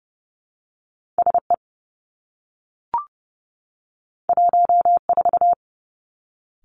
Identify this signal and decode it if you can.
Morse code, heard as a keyed tone, about 30 wpm, 700 Hz. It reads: S E 1 4
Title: Two Challenging Letter-Number Combinations 30wpm